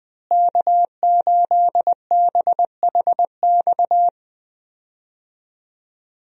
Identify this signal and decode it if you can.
Morse code, heard as a keyed tone, about 20 wpm, 700 Hz. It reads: K8BHX
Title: USA Call Signs - 5 Characters - Repeated 3x - 20wpm